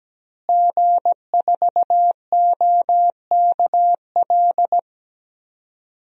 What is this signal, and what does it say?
Morse code, heard as a keyed tone, about 17 wpm, 700 Hz. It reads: G4OKL
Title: DXCC-100 Call Signs - 5 Characters - Repeated 3x - 17wpm